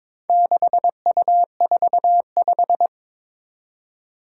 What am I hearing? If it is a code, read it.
Morse code, heard as a keyed tone, about 22 wpm, 700 Hz. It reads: 6U45
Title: Four Challenging Letter-Number Combinations 22wpm